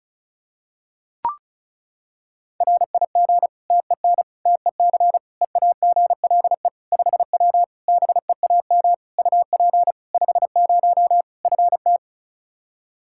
Morse code, heard as a keyed tone, about 34 wpm, 700 Hz. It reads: RIG TEN TEC EAGLE 5W BEAM UP 50 FT
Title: Practice QSO - Exchange 2 - Info - 35wpm